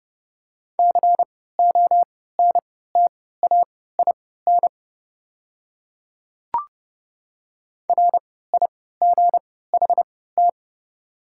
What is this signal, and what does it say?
Morse code, heard as a keyed tone, about 30 wpm, spaced as for 17 wpm, 690 Hz. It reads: CONTAIN RIGHT
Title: 3x Character Spacing - Top 500 Words - 30wpm